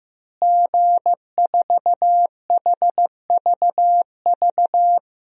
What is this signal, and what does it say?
Morse code, heard as a keyed tone, about 15 wpm, 700 Hz. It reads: G4HVV
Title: DXCC-100 Call Signs - 5 Characters - Repeated 3x - 15wpm